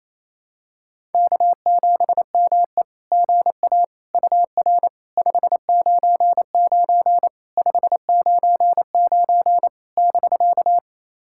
Morse code, heard as a keyed tone, about 27 wpm, 690 Hz. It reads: K7ME GA UR 599 599 <BK>
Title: SOTA QSO Practice 28wpm